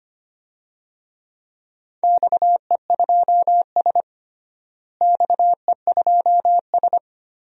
Morse code, heard as a keyed tone, about 25 wpm, 700 Hz. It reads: XE2S XE2S
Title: DXCC-100 Call Signs - 4 Characters - Repeated 2x - 25wpm